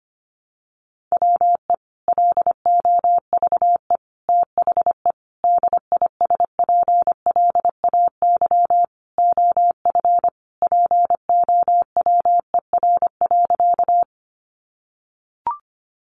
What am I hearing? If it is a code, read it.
Morse code, heard as a keyed tone, about 25 wpm, 700 Hz. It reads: WE LOVE THE DISPLAY OF POWER.
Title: Sentences from Top 1200 Words 25wpm